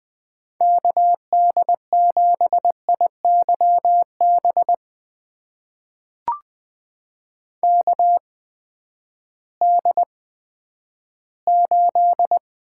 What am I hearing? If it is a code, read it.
Morse code, heard as a keyed tone, about 20 wpm, spaced as for 11 wpm, 700 Hz. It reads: KD7IYB KD8
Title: USA Call Signs - 6 Characters - 8x Character Spacing - 20wpm